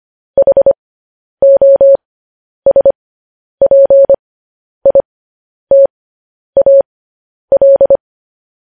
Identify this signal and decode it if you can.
Morse code, heard as a keyed tone, about 25 wpm, spaced as for 10 wpm, 550 Hz. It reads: HOSPITAL
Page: news-headlines-in-morse-code-at-10-wpm